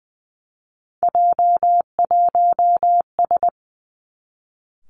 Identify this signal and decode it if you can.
Morse code, heard as a keyed tone, about 20 wpm, 700 Hz. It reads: J1S
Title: Three Challenging Letter-Number Combinations 20wpm